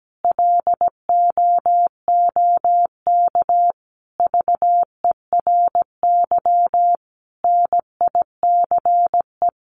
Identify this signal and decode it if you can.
Morse code, heard as a keyed tone, about 17 wpm, 700 Hz. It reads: LOOK VERY NICE
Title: Sentences from Top 1500 Words 17wpm